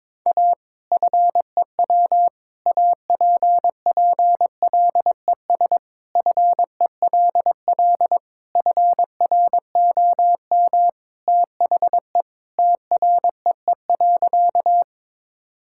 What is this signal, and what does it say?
Morse code, heard as a keyed tone, about 22 wpm, 700 Hz. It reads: A FEW APPLES FELL FROM THE TREE.
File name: practice / 22wpm / Sentences with 2-Word Phrases - 22wpm